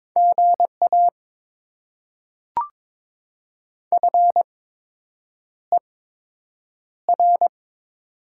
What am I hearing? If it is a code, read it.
Morse code, heard as a keyed tone, about 22 wpm, spaced as for 8 wpm, 700 Hz. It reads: GA FER
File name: Top 20 QSO Elements - 8x Character Spacing - 22wpm